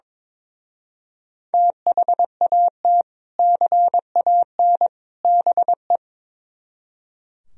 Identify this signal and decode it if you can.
Morse code, heard as a keyed tone, about 24 wpm, 700 Hz. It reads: THAT CAN BE
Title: Speed-Racing - Sets of 3 Words 22wpm